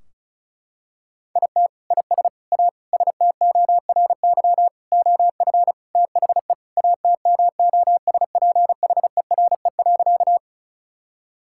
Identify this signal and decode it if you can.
Morse code, heard as a keyed tone, about 34 wpm, 700 Hz. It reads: IT IS A STORY OF THE ATMOSPHERE.
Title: Sentences from Top 1600 Words 35wpm